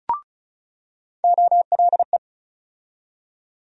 Morse code, heard as a keyed tone, about 36 wpm, 700 Hz. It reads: OLE
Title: USA Call Sign Postfixes - 3 Letters 35wpm